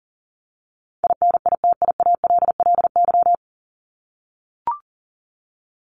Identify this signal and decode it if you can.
Morse code, heard as a keyed tone, about 40 wpm, 700 Hz. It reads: INITIALLY
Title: Top 1251-1500 Words 40wpm